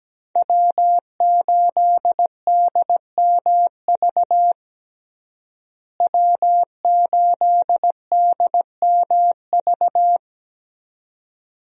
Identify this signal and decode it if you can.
Morse code, heard as a keyed tone, about 17 wpm, 700 Hz. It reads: W8DMV W8DMV
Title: USA Call Signs - 5 Characters - Repeated 2x - 17wpm